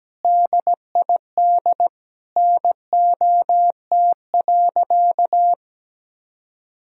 Morse code, heard as a keyed tone, about 17 wpm, 700 Hz. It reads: DID NOT.
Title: Sentences from Top 1000 Words - Review Part 1 - 17wpm